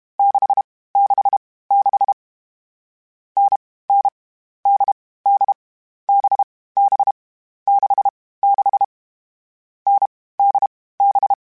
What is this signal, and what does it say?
Morse code, heard as a keyed tone, about 32 wpm, 800 Hz. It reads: 6 6 6 N N D D B B 6 6 N D B
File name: Difficult Combos / Difficult Combos-mp3s / NDB NDB6 @30